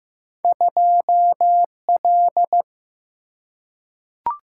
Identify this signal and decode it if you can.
Morse code, heard as a keyed tone, about 15 wpm, 700 Hz. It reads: 2L
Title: Single-Letter Single-Number 15wpm